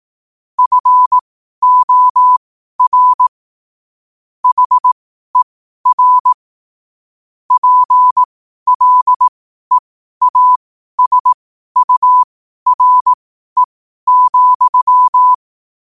Morse code, heard as a keyed tone, about 18 wpm, spaced as for 12 wpm, 1000 Hz. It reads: FOR HER PLEASURE,
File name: qotdfiles / 12wpm